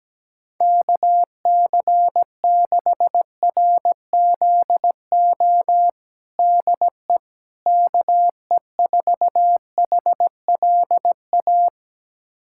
Morse code, heard as a keyed tone, about 17 wpm, 700 Hz. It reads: KC6RZO DE KE4HLA
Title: USA Callsign DE Callsign - 6 Characters - 17wpm